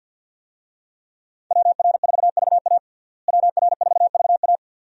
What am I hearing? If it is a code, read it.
Morse code, heard as a keyed tone, about 52 wpm, 700 Hz. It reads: WR4VA WR4VA
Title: USA Call Signs - 5 Characters - Repeated 2x - 50wpm